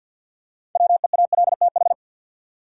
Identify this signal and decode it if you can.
Morse code, heard as a keyed tone, about 53 wpm, 700 Hz. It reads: WEALTH
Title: Six Letter Words - 50wpm